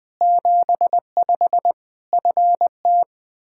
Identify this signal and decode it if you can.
Morse code, heard as a keyed tone, about 20 wpm, 700 Hz. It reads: 75 FT
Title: K1USN SST QSO Practice 20wpm